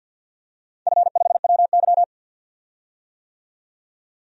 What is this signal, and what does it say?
Morse code, heard as a keyed tone, about 51 wpm, 700 Hz. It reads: F5WY
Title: DXCC-100 Call Signs - 4 Characters - 50wpm